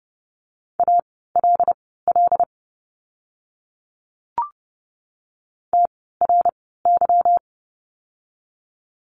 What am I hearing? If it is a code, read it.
Morse code, heard as a keyed tone, about 30 wpm, spaced as for 17 wpm, 700 Hz. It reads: ALL TRY
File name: course / 30:10wpm / 237 Lesson 38f - Letter q - Words 30wpm